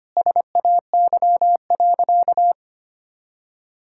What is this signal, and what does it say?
Morse code, heard as a keyed tone, about 25 wpm, 700 Hz. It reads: SAY.
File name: practice / 25wpm / Sentences with 4-Word Phrases - 25wpm